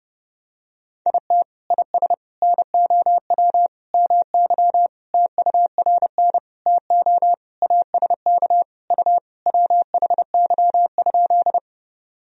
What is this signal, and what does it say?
Morse code, heard as a keyed tone, about 32 wpm, 700 Hz. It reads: IT IS NOW MY TURN TO ASK U WHY?
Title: Sentences from Top 200 Words and Abbreviations 30wpm